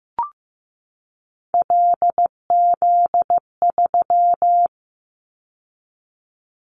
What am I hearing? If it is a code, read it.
Morse code, heard as a keyed tone, about 15 wpm, 700 Hz. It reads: LZ3
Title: Callsign Prefixes for Top 100 DXCC - 15wpm